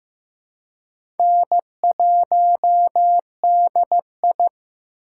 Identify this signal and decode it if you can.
Morse code, heard as a keyed tone, about 15 wpm, 700 Hz. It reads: N1DI
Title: USA Call Signs - 4 Characters - 2x Character Spacing - 15wpm